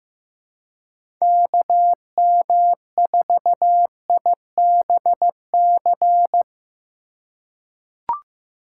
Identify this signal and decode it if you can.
Morse code, heard as a keyed tone, about 15 wpm, 700 Hz. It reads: KM4IBC
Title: USA Call Signs - 6 Characters - 2x Character Spacing - 15wpm